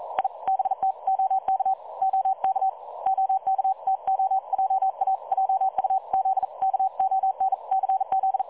Morse code, heard as a keyed tone, about 42 wpm, 770 Hz. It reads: I BT OK OK OK TO JA JUZ KONCZ
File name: dlazaawansowanych / nagraniazpasma / websdr recording 3565.0kHz